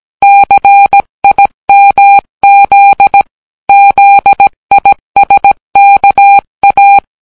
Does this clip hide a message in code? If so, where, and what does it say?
Yes: Morse code from 0.2 to 7.0 s, heard as a keyed tone, about 17 wpm, 800 Hz. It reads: CIMZ ZISKA